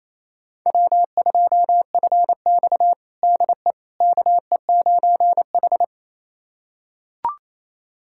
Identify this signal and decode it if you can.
Morse code, heard as a keyed tone, about 28 wpm, 700 Hz. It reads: W2FX DE KE9H
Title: USA Callsign DE Callsign - 4 Characters - 28wpm